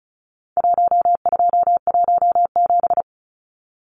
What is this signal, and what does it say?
Morse code, heard as a keyed tone, about 34 wpm, 700 Hz. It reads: 1217
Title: Numbers - Contest Style - 4-Digit 35wpm